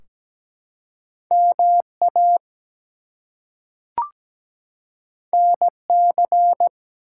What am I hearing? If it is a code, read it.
Morse code, heard as a keyed tone, about 17 wpm, 700 Hz. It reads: MA NC
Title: US States and Canadian Provinces 17wpm